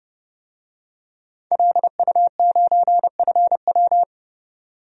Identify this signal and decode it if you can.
Morse code, heard as a keyed tone, about 31 wpm, 700 Hz. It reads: LU9FW
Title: DXCC-100 Call Signs - 5 Characters - 30wpm